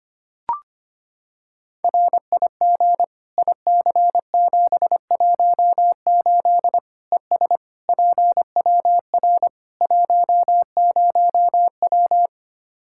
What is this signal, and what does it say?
Morse code, heard as a keyed tone, about 25 wpm, 700 Hz. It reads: RIG IC718 ES PWR 10W
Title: Practice QSO - Exchange 2 - Info - 25wpm